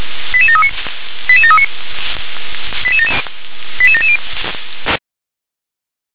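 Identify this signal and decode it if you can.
Five-tone selective call (ZVEI), heard as quick runs of five tones, tones of about 70 ms. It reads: 88739 88739 88770
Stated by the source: ZVEI five-tone selective call received by SDR.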